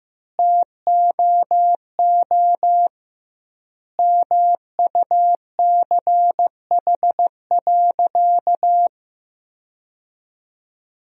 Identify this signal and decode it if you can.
Morse code, heard as a keyed tone, about 15 wpm, 700 Hz. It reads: TOO MUCH.
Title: 2x Word-Spacing - Sentences from Top 400 Words - 15wpm